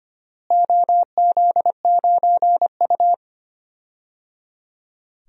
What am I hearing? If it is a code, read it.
Morse code, heard as a keyed tone, about 25 wpm, 700 Hz. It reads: OZ9U
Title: DXCC-100 Call Signs - 4 Characters - Repeated 3x - 25wpm